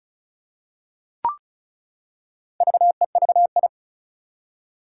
Morse code, heard as a keyed tone, about 36 wpm, 700 Hz. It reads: VEVI
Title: Four Challenging Letter-Number Combinations 35wpm